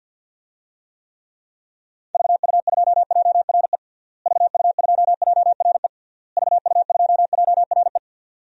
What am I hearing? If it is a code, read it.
Morse code, heard as a keyed tone, about 53 wpm, 700 Hz. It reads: VU2JRE VU2JRE VU2JRE
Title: DXCC-100 Call Signs - 6 Characters - Repeated 3x - 50wpm